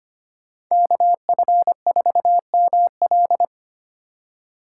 Morse code, heard as a keyed tone, about 33 wpm, 700 Hz. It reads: KF4ML